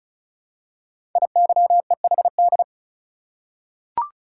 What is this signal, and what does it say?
Morse code, heard as a keyed tone, about 36 wpm, 700 Hz. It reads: IYEHD
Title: Code Groups - 5 Letters - 35wpm